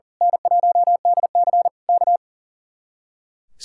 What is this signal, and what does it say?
Morse code, heard as a keyed tone, about 38 wpm, 700 Hz. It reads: N1DC K
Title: Speed-Racing Common QSO Phrases - 40wpm